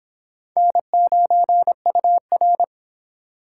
Morse code, heard as a keyed tone, about 29 wpm, 700 Hz. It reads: N9UR